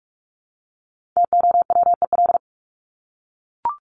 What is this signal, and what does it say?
Morse code, heard as a keyed tone, about 44 wpm, 700 Hz. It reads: TOWEL
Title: Top 1501-1750 Words 45wpm